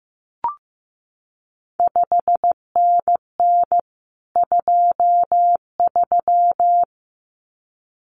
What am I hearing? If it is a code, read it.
Morse code, heard as a keyed tone, about 15 wpm, 700 Hz. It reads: 5NN 23
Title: CQ WW DX Contest Exchange Encore - 15wpm